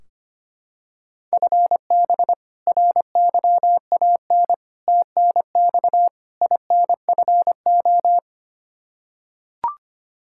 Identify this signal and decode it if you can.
Morse code, heard as a keyed tone, about 25 wpm, 700 Hz. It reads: FB RYAN TNX INFO
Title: Practice QSO - Exchange 3 - More Info - 25wpm